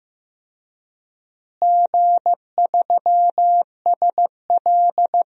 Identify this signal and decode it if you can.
Morse code, heard as a keyed tone, about 15 wpm, 700 Hz. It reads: G3SL
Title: DXCC-100 Call Signs - 4 Characters - Repeated 3x - 15wpm